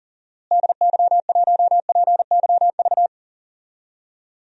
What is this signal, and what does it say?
Morse code, heard as a keyed tone, about 40 wpm, 700 Hz. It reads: DY1PYV
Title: DXCC-100 Call Signs - 6 Characters - Repeated 3x - 40wpm